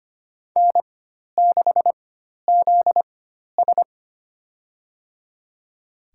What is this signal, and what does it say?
Morse code, heard as a keyed tone, about 25 wpm, spaced as for 12 wpm, 700 Hz. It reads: N6ZS
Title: USA Call Signs - 4 Characters - 4x Character Spacing - 25wpm